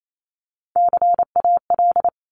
Morse code, heard as a keyed tone, about 28 wpm, 700 Hz. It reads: CAL
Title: Top Three Letter Combinations in Words 28wpm